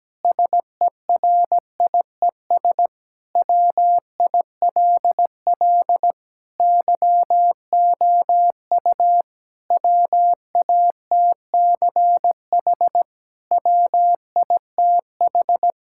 Morse code, heard as a keyed tone, about 17 wpm, 700 Hz. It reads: SERIES WILL YOU WATCH WITH
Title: Sentences from Top 1000 Words - Review Part 5 - 17wpm